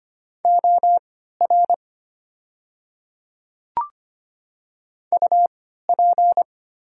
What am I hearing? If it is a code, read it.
Morse code, heard as a keyed tone, about 25 wpm, spaced as for 14 wpm, 700 Hz. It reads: OR UP
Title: 3x Character Spacing - Top 100 Words - 25wpm